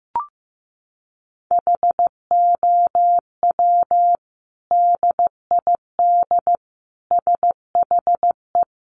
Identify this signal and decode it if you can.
Morse code, heard as a keyed tone, about 15 wpm, 700 Hz. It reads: HOW DID SHE
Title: Sentences from Top 800 Words 15wpm